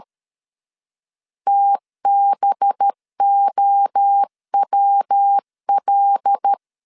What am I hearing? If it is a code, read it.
Morse code, heard as a keyed tone, about 13 wpm, 800 Hz. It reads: TBOWL
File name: LEZIONI / 6 PKX0 / 6 PK 12wpm 2006